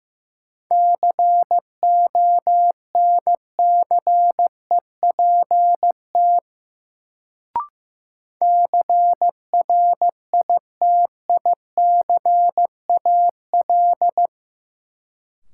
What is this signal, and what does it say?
Morse code, heard as a keyed tone, about 15 wpm, 700 Hz. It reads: CONCEPT CRITICAL
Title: Sentences from Top 1100 Words - New Word Drill Alt - 15wpm